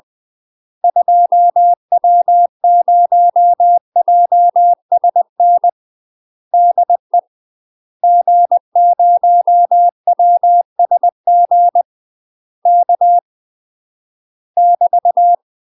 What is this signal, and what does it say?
Morse code, heard as a keyed tone, about 20 wpm, 700 Hz. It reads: 2W0JSN DE G0WSG K <BT>
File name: UK MASTER QSOs TRAINING / 20WPM / UK MASTER QSOs 0017